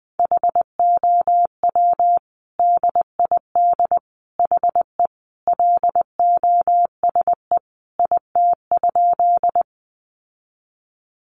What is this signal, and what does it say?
Morse code, heard as a keyed tone, about 20 wpm, 700 Hz. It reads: HOW DID HE LOSE IT?
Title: Sentences from Top 1100 Words 20wpm